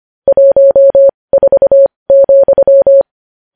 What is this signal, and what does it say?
Morse code, heard as a keyed tone, about 25 wpm, spaced as for 20 wpm, 550 Hz. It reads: 14,
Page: news-headlines-in-morse-code-at-20-wpm